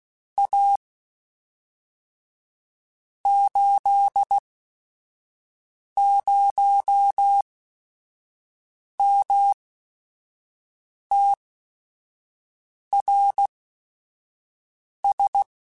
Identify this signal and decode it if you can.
Morse code, heard as a keyed tone, about 16 wpm, spaced as for 5 wpm, 800 Hz. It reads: A 80MTRS
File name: QSOs / QSOs-mp3s / WD0FBT 1@5